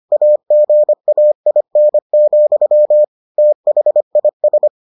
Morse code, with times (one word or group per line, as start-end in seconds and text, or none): 0.1-3.0 AGAIN,
3.4-4.7 THIS